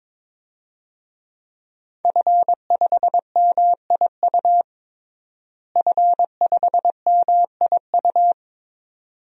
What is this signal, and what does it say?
Morse code, heard as a keyed tone, about 21 wpm, 700 Hz. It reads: F5MIU F5MIU
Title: DXCC-100 Call Signs - 5 Characters - Repeated 2x - 22wpm